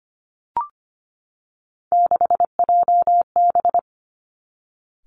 Morse code, heard as a keyed tone, about 25 wpm, 700 Hz. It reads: 6JB
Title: Three Challenging Letter-Number Combinations 25wpm